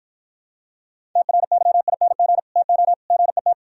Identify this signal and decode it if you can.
Morse code, heard as a keyed tone, about 53 wpm, 700 Hz. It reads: TRYING TO GET